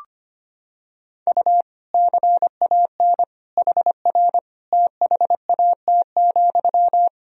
Morse code, heard as a keyed tone, about 25 wpm, 700 Hz. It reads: U CAN HR THAT,
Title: Sentences from Top 500 Words and Abbreviations 25wpm